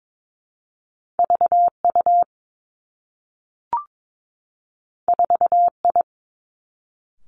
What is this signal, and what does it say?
Morse code, heard as a keyed tone, about 22 wpm, 700 Hz. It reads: V U 4 I
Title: Two Challenging Letter-Number Combinations 22wpm